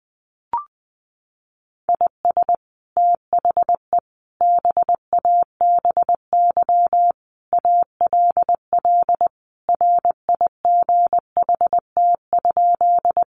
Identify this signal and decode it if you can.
Morse code, heard as a keyed tone, about 20 wpm, 700 Hz. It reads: IS THE BABY ALL RIGHT?
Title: Sentences from Top 1300 Words 20wpm